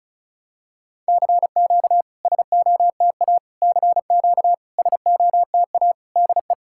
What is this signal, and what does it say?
Morse code, heard as a keyed tone, about 36 wpm, 690 Hz. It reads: CQ SOTA CQ SOTA DE
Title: SOTA QSO Practice 35wpm